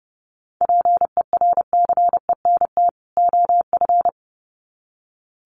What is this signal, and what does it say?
Morse code, heard as a keyed tone, about 30 wpm, 700 Hz. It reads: PERCENT OF
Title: Sets of 2 Words 30wpm